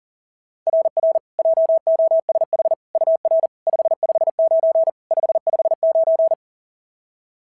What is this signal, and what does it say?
Morse code, heard as a keyed tone, about 40 wpm, 690 Hz. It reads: RR JOSH UR 559 559
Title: SOTA QSO Practice 40wpm